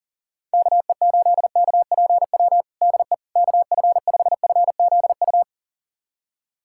Morse code, heard as a keyed tone, about 40 wpm, 700 Hz. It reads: KE8KPW DE KF5FZU
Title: USA Callsign DE Callsign 40wpm